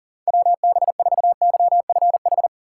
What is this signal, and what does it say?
Morse code, heard as a keyed tone, about 40 wpm, 700 Hz. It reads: WB4YFH
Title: USA Call Signs - 6 Characters - 3x Character Spacing - 40wpm